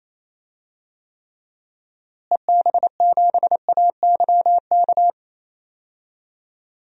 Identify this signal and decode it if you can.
Morse code, heard as a keyed tone, about 28 wpm, 700 Hz. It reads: EB7AYK
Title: DXCC-100 Call Signs - 6 Characters - Repeated 3x - 28wpm